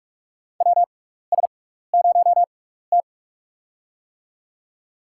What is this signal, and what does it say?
Morse code, heard as a keyed tone, about 44 wpm, spaced as for 15 wpm, 700 Hz. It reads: WS0T
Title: USA Call Signs - 4 Characters - 6x Character Spacing - 45wpm